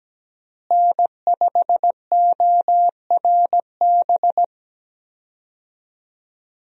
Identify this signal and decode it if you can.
Morse code, heard as a keyed tone, about 17 wpm, 700 Hz. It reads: N5ORB
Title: USA Call Signs - 5 Characters - Repeated 2x - 17wpm